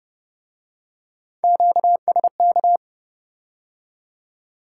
Morse code, heard as a keyed tone, about 32 wpm, 700 Hz. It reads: QSK